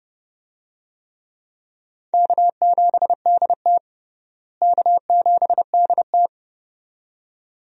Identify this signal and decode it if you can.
Morse code, heard as a keyed tone, about 32 wpm, 700 Hz. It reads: K7DT K7DT
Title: USA Call Signs - 4 Characters - Repeated 2x - 30wpm